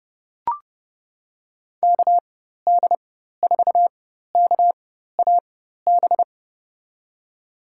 Morse code, heard as a keyed tone, about 30 wpm, spaced as for 14 wpm, 700 Hz. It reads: KD4KAB
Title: USA Call Signs - 4x Character Spacing - 30wpm